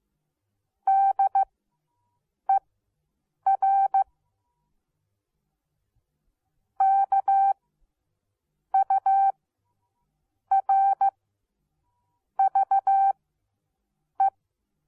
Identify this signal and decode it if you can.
Morse code, heard as a keyed tone, about 15 wpm, spaced as for 6 wpm, 780 Hz. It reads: DER KURVE